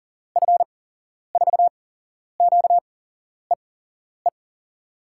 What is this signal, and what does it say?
Morse code, heard as a keyed tone, about 38 wpm, spaced as for 11 wpm, 700 Hz. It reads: F4QEE
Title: USA Call Signs - 6 Characters - 8x Character Spacing - 40wpm